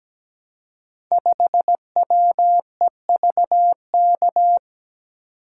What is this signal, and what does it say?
Morse code, heard as a keyed tone, about 17 wpm, 700 Hz. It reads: 5WEVK
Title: Code Groups - 5 Letter:Numbers - 17wpm